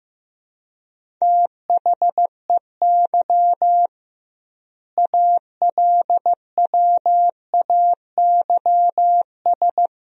Morse code, heard as a keyed tone, about 15 wpm, 700 Hz. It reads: THEY ALWAYS
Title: 2x Word-Spacing - Sentences from Top 300 Words - 15wpm